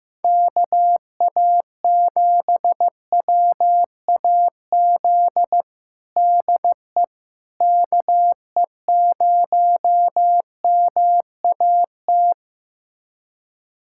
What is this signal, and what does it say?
Morse code, heard as a keyed tone, about 15 wpm, 700 Hz. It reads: KA7WAZ DE KE0MAT
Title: USA Callsign DE Callsign - 6 Characters - 15wpm